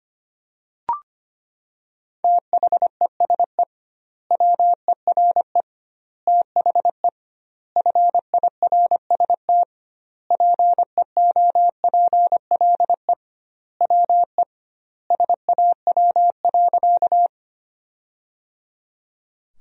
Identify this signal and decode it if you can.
Morse code, heard as a keyed tone, about 25 wpm, 700 Hz. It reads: THESE WERE THE FIRST PEOPLE WE SAW.
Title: 2x Word-Spacing - Sentences from Top 300 Words - 25wpm